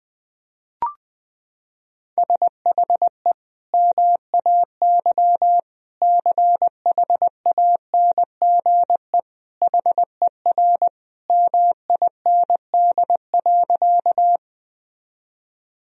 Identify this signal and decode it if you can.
Morse code, heard as a keyed tone, about 21 wpm, 700 Hz. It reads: SHE MAY CHANGE HER MIND.
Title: Sentences from Top 1000 Words - Review Part 1 - 20wpm